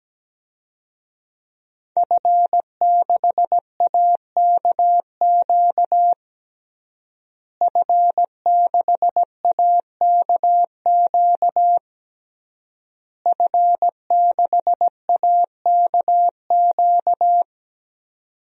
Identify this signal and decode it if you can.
Morse code, heard as a keyed tone, about 17 wpm, 700 Hz. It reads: F6AKQ F6AKQ F6AKQ
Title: DXCC-100 Call Signs - 5 Characters - Repeated 3x - 17wpm